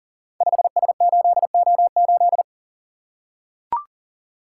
Morse code, heard as a keyed tone, about 38 wpm, 700 Hz. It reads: 5S8O8
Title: Code Groups - 5 Letter:Numbers - 40wpm